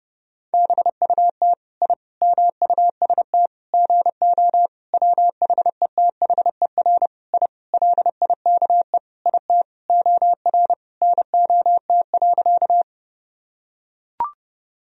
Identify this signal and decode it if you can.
Morse code, heard as a keyed tone, about 30 wpm, 700 Hz. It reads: BUT I MUST GO WHETHER I LIKE IT OR NOT.
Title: Sentences from Top 1000 Words - Review Part 3 - 30wpm